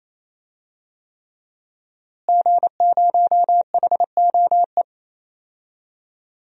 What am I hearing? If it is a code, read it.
Morse code, heard as a keyed tone, about 28 wpm, 700 Hz. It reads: G0HOE
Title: DXCC-100 Call Signs - 5 Characters - Repeated 3x - 28wpm